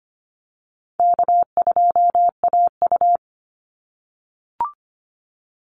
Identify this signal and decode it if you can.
Morse code, heard as a keyed tone, about 25 wpm, 700 Hz. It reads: K2AU
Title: Easy Call Signs - 25wpm